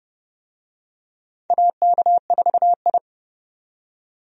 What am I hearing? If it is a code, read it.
Morse code, heard as a keyed tone, about 30 wpm, 700 Hz. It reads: AK4I
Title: USA Call Signs - 4 Characters - 30wpm